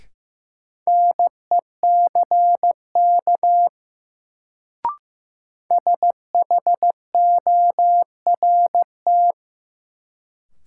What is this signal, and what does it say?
Morse code, heard as a keyed tone, about 15 wpm, 700 Hz. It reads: NECK SHORT
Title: Sets of 5 Words Encore - New Word Drill - 15wpm